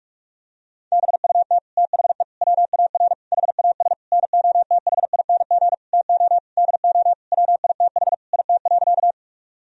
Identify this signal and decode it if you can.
Morse code, heard as a keyed tone, about 46 wpm, 700 Hz. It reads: BUT THE WAR HAS NOTHING TO DO WITH IT.
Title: Sentences from Top 1000 Words - Review Part 2 - 45wpm